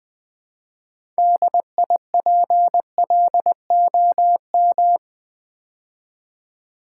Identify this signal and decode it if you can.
Morse code, heard as a keyed tone, about 20 wpm, 700 Hz. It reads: DIPLOM
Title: Build A Word - 2 Letters At A Time - 8-Letter Words - 20wpm